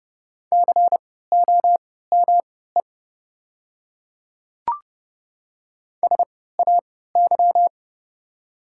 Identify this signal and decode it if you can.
Morse code, heard as a keyed tone, about 30 wpm, spaced as for 17 wpm, 700 Hz. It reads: COME SAY
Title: 3x Character Spacing - Top 200 Words - 30wpm